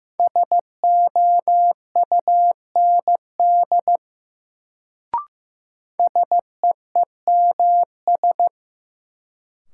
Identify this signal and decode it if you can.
Morse code, heard as a keyed tone, about 15 wpm, 700 Hz. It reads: SOUND SEEMS
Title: Sets of 3 Words - New Word Drill - 15wpm